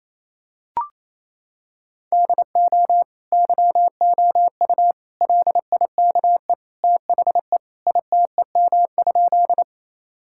Morse code, heard as a keyed tone, about 28 wpm, 700 Hz. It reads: DO YOU LIKE THE ITEM?
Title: Sentences from Top 1000 Words - Review Part 6 - 28wpm